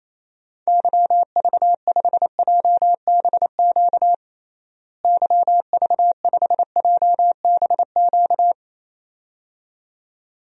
Morse code, heard as a keyed tone, about 28 wpm, 700 Hz. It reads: YV5JBQ YV5JBQ
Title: DXCC-100 Call Signs - 6 Characters - Repeated 2x - 28wpm